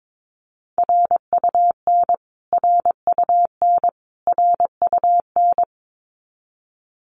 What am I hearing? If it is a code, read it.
Morse code, heard as a keyed tone, about 22 wpm, 700 Hz. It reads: RUN RUN RUN
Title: Top 500 Words Repeated 22wpm